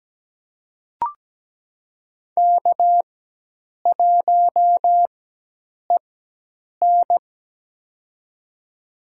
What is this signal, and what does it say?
Morse code, heard as a keyed tone, about 17 wpm, spaced as for 8 wpm, 700 Hz. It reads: K1EN
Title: USA Call Signs - 4 Characters - 4x Character Spacing - 17wpm